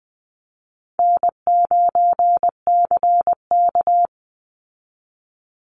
Morse code, heard as a keyed tone, about 20 wpm, 700 Hz. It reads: N9CK
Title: Easy Call Signs Repeated 3x - 20wpm